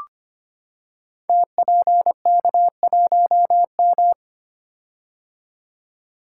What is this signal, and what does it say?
Morse code, heard as a keyed tone, about 25 wpm, 700 Hz. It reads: TPK1M
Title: Code Groups - 5 Letter:Numbers - 25wpm